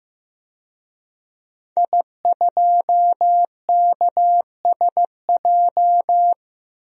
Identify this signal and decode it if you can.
Morse code, heard as a keyed tone, about 15 wpm, 700 Hz. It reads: I2KSJ
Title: DXCC-100 Call Signs - 5 Characters - 15wpm